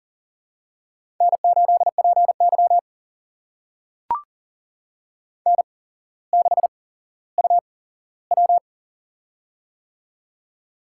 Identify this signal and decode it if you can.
Morse code, heard as a keyed tone, about 40 wpm, spaced as for 17 wpm, 700 Hz. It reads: N8PY N6UW
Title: USA Call Signs - 4 Characters - 8x Character Spacing - 40wpm